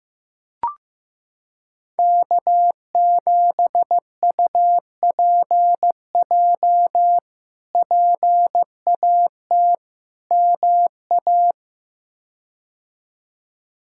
Morse code, heard as a keyed tone, about 15 wpm, 700 Hz. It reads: K7UPJ PAT MA
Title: K1USN SST QSO Practice - 15wpm